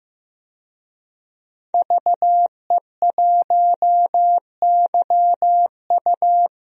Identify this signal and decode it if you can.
Morse code, heard as a keyed tone, about 15 wpm, 700 Hz. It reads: VE1YU
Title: DXCC-100 Call Signs - 5 Characters - 15wpm